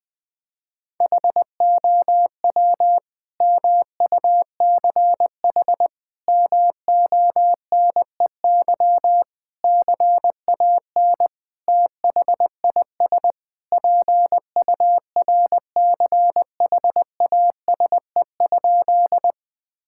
Morse code, heard as a keyed tone, about 20 wpm, 700 Hz. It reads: HOW MUCH MONEY CAN THIS PURCHASE?
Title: Sentences from Top 1000 Words - Review Part 5 - 20wpm